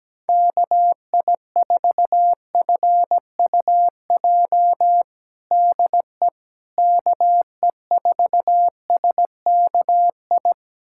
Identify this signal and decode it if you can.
Morse code, heard as a keyed tone, about 17 wpm, 700 Hz. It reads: KI4FUJ DE KE4SKI
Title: USA Callsign DE Callsign - 6 Characters - 17wpm